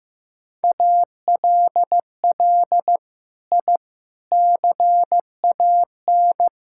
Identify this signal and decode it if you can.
Morse code, heard as a keyed tone, about 16 wpm, 700 Hz. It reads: ALL I CAN
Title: Sentences from Top 1000 Words - Review Part 6 - 15wpm